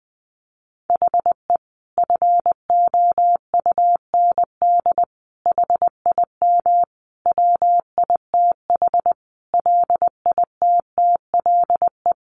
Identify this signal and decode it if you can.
Morse code, heard as a keyed tone, about 20 wpm, 700 Hz. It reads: HE FOUND HIM WITH LITTLE
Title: Sentences from Top 1500 Words 20wpm